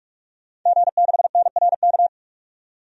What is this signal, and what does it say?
Morse code, heard as a keyed tone, about 44 wpm, 700 Hz. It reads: G6NRK
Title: DXCC-100 Call Signs - 5 Characters - 45wpm